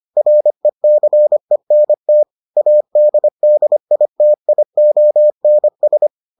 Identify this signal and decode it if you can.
Morse code, heard as a keyed tone, about 25 wpm, 600 Hz. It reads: RECENT ADDITIONS